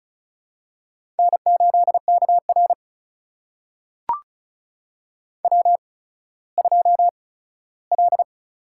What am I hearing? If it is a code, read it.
Morse code, heard as a keyed tone, about 35 wpm, spaced as for 17 wpm, 700 Hz. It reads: N8KR W2L
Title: USA Call Signs - 4 Characters - 8x Character Spacing - 35wpm